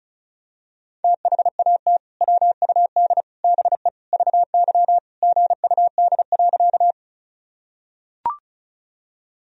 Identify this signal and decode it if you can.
Morse code, heard as a keyed tone, about 36 wpm, 700 Hz. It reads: THAT WUD BE VY GUD.
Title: Sentences from Top 200 Words and Abbreviations 35wpm